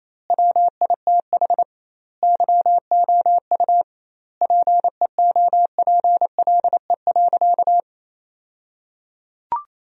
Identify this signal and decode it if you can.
Morse code, heard as a keyed tone, about 28 wpm, 700 Hz. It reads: WITH YOU PEOPLE.
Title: 2x Word-Spacing - Sentences from Top 300 Words - 28wpm